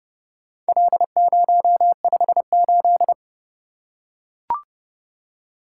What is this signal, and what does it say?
Morse code, heard as a keyed tone, about 30 wpm, 700 Hz. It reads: L058